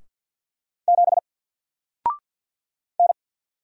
Morse code, heard as a keyed tone, about 48 wpm, 700 Hz. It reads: 7 N
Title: Single Letter-Number Alt 50wpm